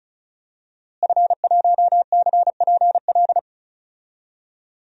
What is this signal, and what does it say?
Morse code, heard as a keyed tone, about 34 wpm, 700 Hz. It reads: F1CPL
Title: DXCC-100 Call Signs - 5 Characters - Repeated 2x - 35wpm